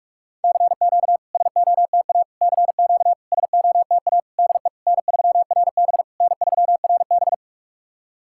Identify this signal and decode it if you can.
Morse code, heard as a keyed tone, about 44 wpm, 690 Hz. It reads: CQ SOTA CQ SOTA DE N3RB N3RB
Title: SOTA QSO Practice 45wpm